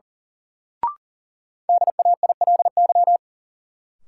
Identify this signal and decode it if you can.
Morse code, heard as a keyed tone, about 37 wpm, 700 Hz. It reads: DAILY
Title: Sentences from Top 1100 Words - New Word Drill Alt - 40wpm